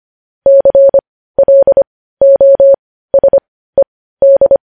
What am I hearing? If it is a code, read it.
Morse code, heard as a keyed tone, about 25 wpm, spaced as for 15 wpm, 550 Hz. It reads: CLOSED
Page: News Headlines in Morse Code at 15 WPM